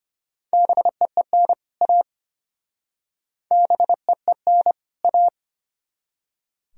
Morse code, heard as a keyed tone, about 33 wpm, 700 Hz. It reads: BEEN A BEEN A